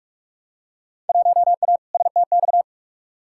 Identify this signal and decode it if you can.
Morse code, heard as a keyed tone, about 43 wpm, 700 Hz. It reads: 1A STX
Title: Field Day Exchanges Encore - 45wpm